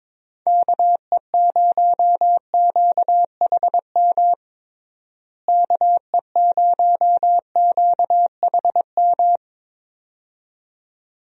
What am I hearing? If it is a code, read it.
Morse code, heard as a keyed tone, about 22 wpm, 700 Hz. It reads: KE0QHM KE0QHM
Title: USA Call Signs - 6 Characters - Repeated 2x - 22wpm